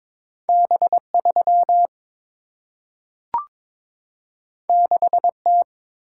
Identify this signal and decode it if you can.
Morse code, heard as a keyed tone, about 22 wpm, 700 Hz. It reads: B 3 6 T
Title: Single-Letter Single-Number 22wpm